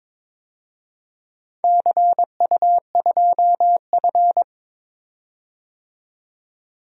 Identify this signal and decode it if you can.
Morse code, heard as a keyed tone, about 22 wpm, 700 Hz. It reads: CU2F
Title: DXCC-100 Call Signs - 4 Characters - 22wpm